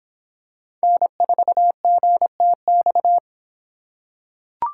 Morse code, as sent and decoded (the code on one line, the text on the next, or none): -. ....- --. - -..-
N4GTX